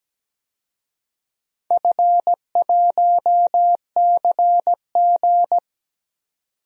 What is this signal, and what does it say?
Morse code, heard as a keyed tone, about 17 wpm, 700 Hz. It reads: F1CG
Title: DXCC-100 Call Signs - 4 Characters - 17wpm